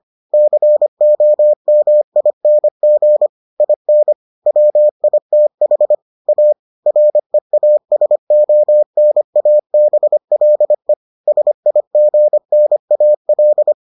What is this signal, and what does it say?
Morse code, heard as a keyed tone, about 25 wpm, 600 Hz. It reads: COMING IN WITH A REASONABLE SIGNAL